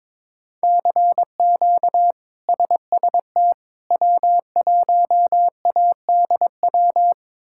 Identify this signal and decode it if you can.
Morse code, heard as a keyed tone, about 22 wpm, 700 Hz. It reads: CQ SST W1ADW
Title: K1USN SST QSO Practice - 22wpm